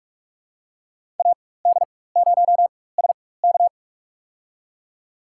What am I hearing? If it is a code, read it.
Morse code, heard as a keyed tone, about 46 wpm, spaced as for 21 wpm, 700 Hz. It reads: AD0SK
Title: USA Call Signs - 5 Characters - 4x Character Spacing - 45wpm